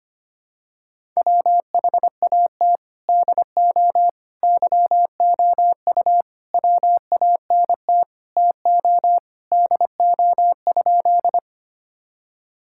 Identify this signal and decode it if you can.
Morse code, heard as a keyed tone, about 25 wpm, 700 Hz. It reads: WHAT DO YOU WANT TO DO?
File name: practice / 25wpm / Sentences with 3-Word Phrases - 25wpm